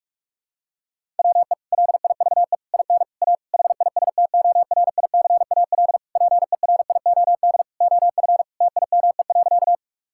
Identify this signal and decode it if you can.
Morse code, heard as a keyed tone, about 46 wpm, 700 Hz. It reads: WE LIVE IN A HISTORICAL PERIOD OF TIME.
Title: Sentences from Top 1000 Words - Review Part 5 - 45wpm